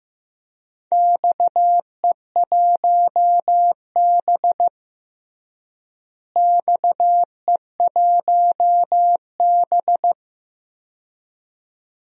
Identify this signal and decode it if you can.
Morse code, heard as a keyed tone, about 15 wpm, 700 Hz. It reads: XE1B XE1B
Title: DXCC-100 Call Signs - 4 Characters - Repeated 2x - 15wpm